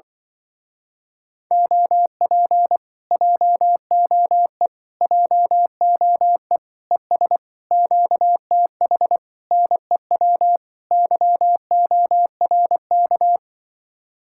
Morse code, heard as a keyed tone, about 23 wpm, 700 Hz. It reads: OP JOE JOE ES QTH NEW YORK
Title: Speed-Racing Common QSO Phrases - 20wpm